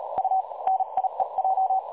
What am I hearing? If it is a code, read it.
Morse code, heard as a keyed tone, about 37 wpm, spaced as for 32 wpm, 760 Hz. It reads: U NIEJ
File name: dlazaawansowanych / nagraniazpasma / websdr recording 3565.0kHz